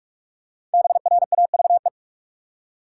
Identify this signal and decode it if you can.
Morse code, heard as a keyed tone, about 44 wpm, 700 Hz. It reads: BRAVE
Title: Five Letter Words - 45wpm